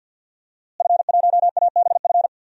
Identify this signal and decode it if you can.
Morse code, heard as a keyed tone, about 52 wpm, 700 Hz. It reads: F1ABF
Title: DXCC-100 Call Signs - 5 Characters - Repeated 3x - 50wpm